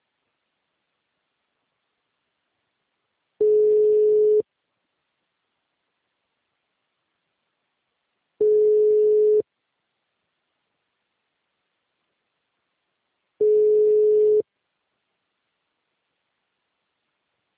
Ringback tone (Europe, 425 Hz).
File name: out- -2011- - - .37366